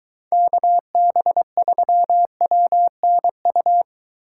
Morse code, heard as a keyed tone, about 23 wpm, 700 Hz. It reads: KB3WNU